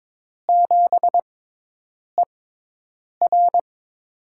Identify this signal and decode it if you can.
Morse code, heard as a keyed tone, about 22 wpm, spaced as for 8 wpm, 700 Hz. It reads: 7ER
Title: USA Call Signs - 6x Character Spacing - 22wpm